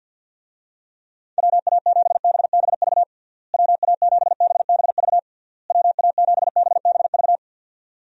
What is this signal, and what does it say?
Morse code, heard as a keyed tone, about 51 wpm, 700 Hz. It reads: WA7BBV WA7BBV WA7BBV
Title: USA Call Signs - 6 Characters - Repeated 3x - 50wpm